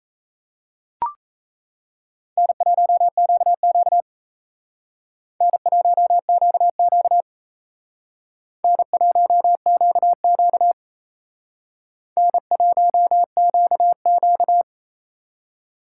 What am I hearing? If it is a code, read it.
Morse code, heard as a keyed tone, about 35 wpm, 700 Hz. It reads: N1QQ N1QQ N1QQ N1QQ
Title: Speed Racing - USA 1x2 Call Signs 28wpm